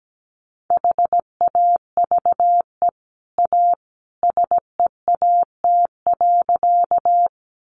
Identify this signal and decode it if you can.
Morse code, heard as a keyed tone, about 17 wpm, 700 Hz. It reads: HAVE A SEAT.
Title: Sentences from Top 1500 Words 17wpm